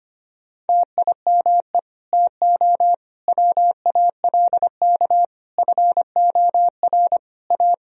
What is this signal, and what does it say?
Morse code, heard as a keyed tone, about 25 wpm, 700 Hz. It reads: TIME TO WALK FOR A
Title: Sentences from Top 1000 Words - Review Part 1 - 25wpm